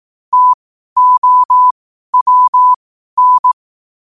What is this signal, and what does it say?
Morse code, heard as a keyed tone, about 18 wpm, spaced as for 12 wpm, 1000 Hz. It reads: TOWN
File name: qotdfiles / 12wpm